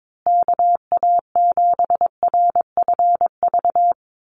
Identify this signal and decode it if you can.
Morse code, heard as a keyed tone, about 22 wpm, 700 Hz. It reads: KA7RFV
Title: USA Call Signs Repeated 2x 22wpm